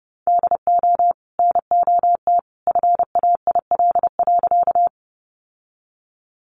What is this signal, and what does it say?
Morse code, heard as a keyed tone, about 30 wpm, 700 Hz. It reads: DO NOT FAIL.
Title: Sentences from Top 1100 Words 30wpm